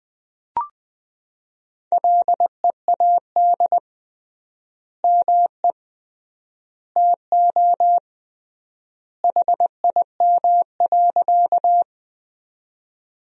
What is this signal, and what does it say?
Morse code, heard as a keyed tone, about 20 wpm, 700 Hz. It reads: LEAD ME TO HIM.
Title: 3x Word-Spacing - Sentences from Top 500 Words - 20wpm